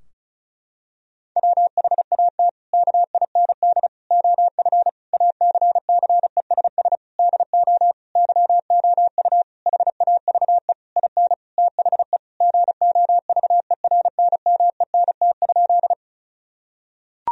Morse code, heard as a keyed tone, about 36 wpm, 700 Hz. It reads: WHAT KIND OF ACCESS DO YOU HAVE IN THE GOVERNMENT?
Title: Sentences from Top 1000 Words - Review Part 4 - 35wpm